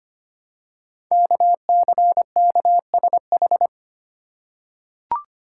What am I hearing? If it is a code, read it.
Morse code, heard as a keyed tone, about 25 wpm, 700 Hz. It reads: KCKSH
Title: Code Groups - 5 Letters - 25wpm